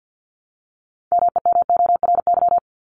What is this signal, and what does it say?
Morse code, heard as a keyed tone, about 50 wpm, 700 Hz. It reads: MEMORY
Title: Top 1251-1500 Words 50wpm